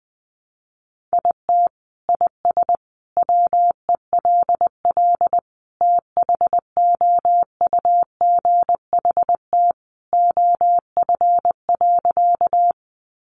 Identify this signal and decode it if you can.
Morse code, heard as a keyed tone, about 20 wpm, 700 Hz. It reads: IT IS WELL THOUGHT OF.
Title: Sentences from Top 300 Words 20wpm